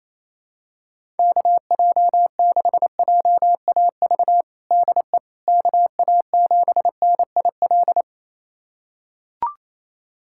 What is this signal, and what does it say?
Morse code, heard as a keyed tone, about 28 wpm, 700 Hz. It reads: KJ6JAV DE KA7NIL
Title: USA Callsign DE Callsign 28wpm